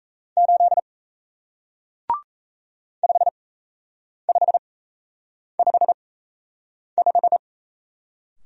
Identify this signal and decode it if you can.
Morse code, heard as a keyed tone, about 37 wpm, spaced as for 8 wpm, 700 Hz. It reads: 8 5555